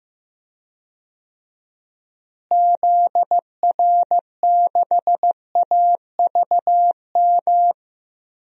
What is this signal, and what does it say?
Morse code, heard as a keyed tone, about 16 wpm, 700 Hz. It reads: ZR6AVM
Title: DXCC-100 Call Signs - 6 Characters - Repeated 3x - 15wpm